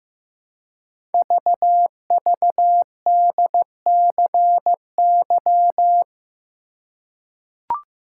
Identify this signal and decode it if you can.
Morse code, heard as a keyed tone, about 15 wpm, 700 Hz. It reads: VVDCY
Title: Code Groups - 5 Letters - 15wpm